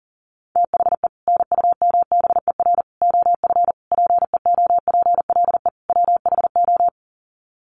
Morse code, heard as a keyed tone, about 40 wpm, 700 Hz. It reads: THE NUMBER OF PEOPLE WHO
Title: Sets of 5 Words Encore 40wpm